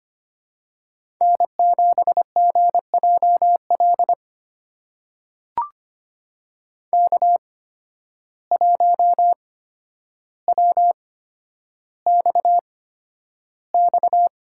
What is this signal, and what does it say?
Morse code, heard as a keyed tone, about 25 wpm, spaced as for 11 wpm, 700 Hz. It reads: N7GJL K1WXX
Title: USA Call Signs - 5 Characters - 8x Character Spacing - 25wpm